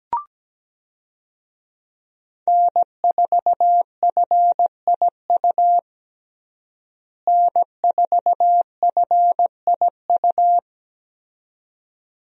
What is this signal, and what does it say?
Morse code, heard as a keyed tone, about 17 wpm, 700 Hz. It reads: N4FIU N4FIU
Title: USA Call Signs - 5 Characters - Repeated 2x - 17wpm